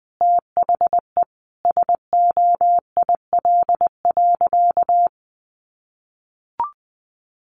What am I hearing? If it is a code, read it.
Morse code, heard as a keyed tone, about 20 wpm, 700 Hz. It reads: THE SOIL.
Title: Sentences from Top 1500 Words 20wpm